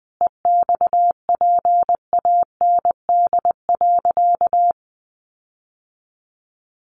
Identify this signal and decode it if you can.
Morse code, heard as a keyed tone, about 20 wpm, 700 Hz. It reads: EXPAND.
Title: Sentences from Top 1400 Words 20wpm